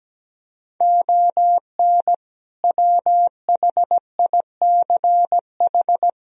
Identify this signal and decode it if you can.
Morse code, heard as a keyed tone, about 17 wpm, 700 Hz. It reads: ON WHICH
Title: Sentences from Top 1000 Words - Review Part 1 - 17wpm